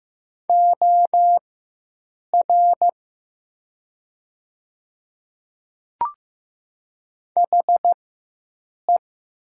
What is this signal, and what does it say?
Morse code, heard as a keyed tone, about 15 wpm, spaced as for 7 wpm, 700 Hz. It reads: OR HE
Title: 4x Character Spacing - Top 100 Words - 15wpm